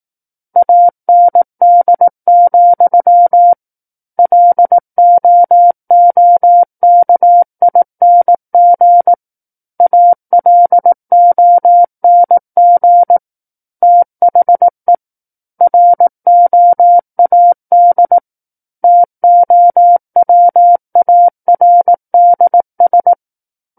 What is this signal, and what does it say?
Morse code, heard as a keyed tone, about 18 wpm, 700 Hz. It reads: AND, LOOKING ALONG THE ROAD TOWARDS
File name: 18 WPM